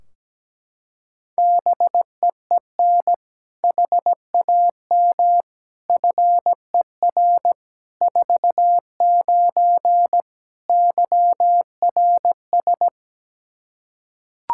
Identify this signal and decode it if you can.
Morse code, heard as a keyed tone, about 17 wpm, 700 Hz. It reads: BEEN HAM FER 49 YRS
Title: Practice QSO - Exchange 3 - More Info - 17wpm